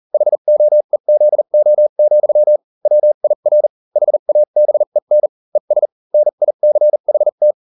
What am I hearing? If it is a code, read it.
Morse code, heard as a keyed tone, about 40 wpm, spaced as for 35 wpm, 600 Hz. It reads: HOEZO, WIR HABEN ES NICHT